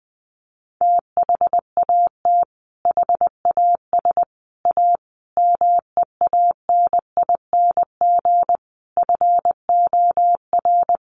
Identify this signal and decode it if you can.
Morse code, heard as a keyed tone, about 20 wpm, 700 Hz. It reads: THAT HAS A MEANING FOR
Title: Sentences from Top 900 Words 20wpm